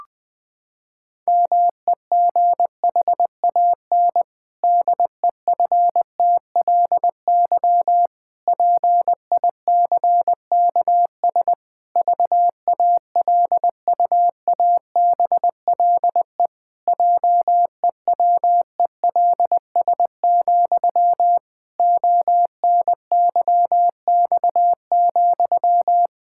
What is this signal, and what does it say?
Morse code, heard as a keyed tone, about 20 wpm, 700 Hz. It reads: MEGHAN DEFTLY PICKS VALUABLE JEWELS, ONYX,
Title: Pangram Challenge v2 - 20wpm